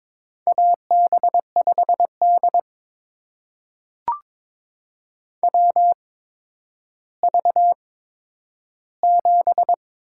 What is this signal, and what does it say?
Morse code, heard as a keyed tone, about 22 wpm, spaced as for 11 wpm, 700 Hz. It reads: AB5D WV7
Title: USA Call Signs - 4 Characters - 8x Character Spacing - 22wpm